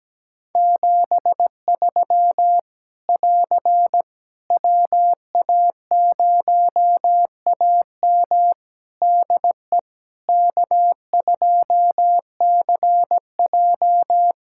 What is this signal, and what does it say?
Morse code, heard as a keyed tone, about 17 wpm, 700 Hz. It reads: 73 <AR> WA0AM DE K2CJ
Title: K1USN SST QSO Practice 17wpm